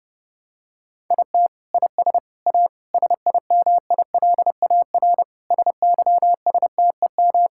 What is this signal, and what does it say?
Morse code, heard as a keyed tone, about 29 wpm, 700 Hz. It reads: IT IS A SIMILAR SYSTEM
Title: Sentences from Top 1000 Words - Review Part 5 - 30wpm